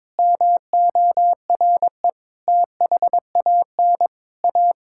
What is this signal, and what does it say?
Morse code, heard as a keyed tone, about 22 wpm, 700 Hz. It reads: MORE THAN A
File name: practice / 22wpm / Sentences with 4-Word Phrases - 22wpm